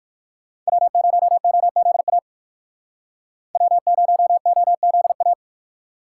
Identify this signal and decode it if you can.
Morse code, heard as a keyed tone, about 56 wpm, 700 Hz. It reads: W0OZA W0OZA
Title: Speed Racing - USA 1x3 Call Signs 45wpm